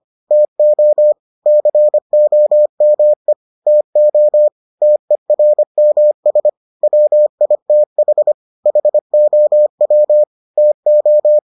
Morse code, heard as a keyed tone, about 25 wpm, 600 Hz. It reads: TO COME TO TERMS WITH HOW TO